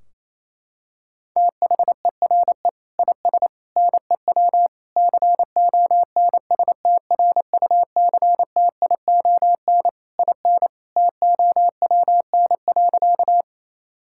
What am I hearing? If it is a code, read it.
Morse code, heard as a keyed tone, about 28 wpm, 700 Hz. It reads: THERE IS NEW CONSTRUCTION IN TOWN.
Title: Sentences from Top 1600 Words 28wpm